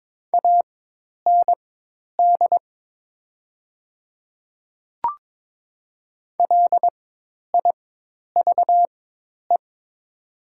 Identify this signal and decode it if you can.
Morse code, heard as a keyed tone, about 22 wpm, spaced as for 10 wpm, 700 Hz. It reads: AND LIVE
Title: 4x Character Spacing - Top 300 Words - 22wpm